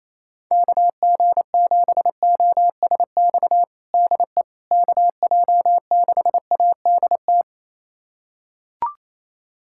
Morse code, heard as a keyed tone, about 27 wpm, 700 Hz. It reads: KG7OSX DE KJ6ADT
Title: USA Callsign DE Callsign 28wpm